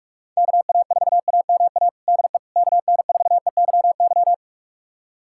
Morse code, heard as a keyed tone, about 43 wpm, 700 Hz. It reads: KA4AMA DE KN4EYY
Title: USA Callsign DE Callsign 45wpm